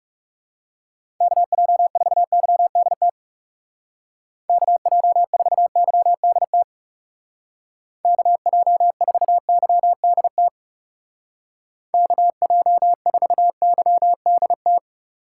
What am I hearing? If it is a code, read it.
Morse code, heard as a keyed tone, about 40 wpm, 700 Hz. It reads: KJ4YDT KJ4YDT KJ4YDT KJ4YDT